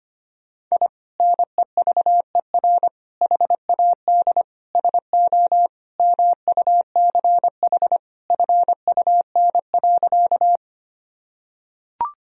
Morse code, heard as a keyed tone, about 25 wpm, 700 Hz. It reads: I NEVER HAD SO MUCH FUN.
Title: Sentences from Top 1000 Words - Review Part 4 - 25wpm